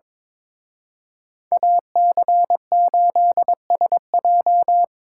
Morse code, heard as a keyed tone, about 25 wpm, 700 Hz. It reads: AC8SJ
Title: Speed Racing - USA 2x2 Call Signs 22wpm